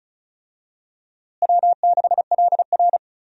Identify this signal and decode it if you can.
Morse code, heard as a keyed tone, about 34 wpm, 700 Hz. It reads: W6LR
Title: USA Call Signs - 4 Characters - Repeated 3x - 35wpm